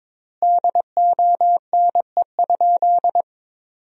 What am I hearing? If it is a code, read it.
Morse code, heard as a keyed tone, about 22 wpm, 700 Hz. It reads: DONE?
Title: 3x Word-Spacing - Sentences from Top 500 Words - 22wpm